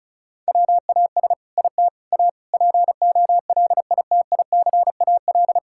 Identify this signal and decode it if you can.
Morse code, heard as a keyed tone, about 34 wpm, 700 Hz. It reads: WAS IT A POLITICAL
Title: Sentences from Top 1000 Words - Review Part 5 - 35wpm